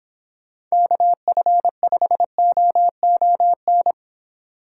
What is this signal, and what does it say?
Morse code, heard as a keyed tone, about 26 wpm, 700 Hz. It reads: KF5OON